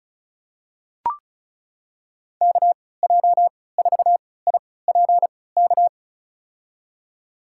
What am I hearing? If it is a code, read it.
Morse code, heard as a keyed tone, about 36 wpm, spaced as for 20 wpm, 690 Hz. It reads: KJ4IPK
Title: USA Call Signs - 6 Characters - 3x Character Spacing - 35wpm